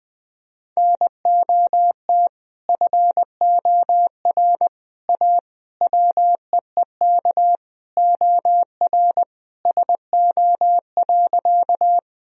Morse code, heard as a keyed tone, about 20 wpm, 700 Hz. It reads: NOT FOR A WEEK OR SO.
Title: Sentences from Top 1000 Words - Review Part 2 - 20wpm